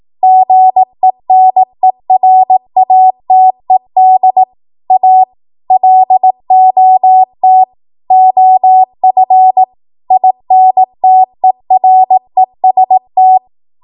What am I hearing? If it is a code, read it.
Morse code, heard as a keyed tone, about 18 wpm, 750 Hz. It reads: GENERATED A LOT OF INTEREST
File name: Morse / 18WPM